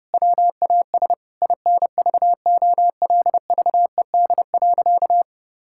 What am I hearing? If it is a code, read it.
Morse code, heard as a keyed tone, about 30 wpm, 700 Hz. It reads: WAS INVOLVED.
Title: Sentences from Top 1000 Words - Review Part 5 - 30wpm